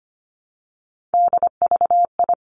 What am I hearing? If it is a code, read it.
Morse code, heard as a keyed tone, about 25 wpm, 700 Hz. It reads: DVI
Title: Three Challenging Letter-Number Combinations 25wpm